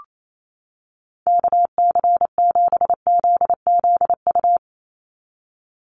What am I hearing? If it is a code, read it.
Morse code, heard as a keyed tone, about 28 wpm, 700 Hz. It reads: KC7ZZU
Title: USA Call Signs 28wpm